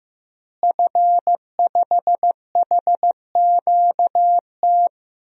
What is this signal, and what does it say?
Morse code, heard as a keyed tone, about 15 wpm, 700 Hz. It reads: F5HQT
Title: DXCC-100 Call Signs - 5 Characters - Repeated 2x - 15wpm